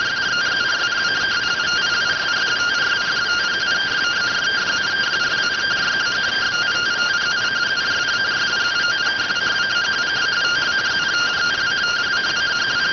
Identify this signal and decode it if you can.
Two-tone teleprinter signal, about 170 Hz shift.